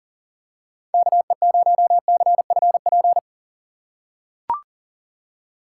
Morse code, heard as a keyed tone, about 40 wpm, 700 Hz. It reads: KE0CFP
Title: USA Call Signs - 6x Character Spacing - 40wpm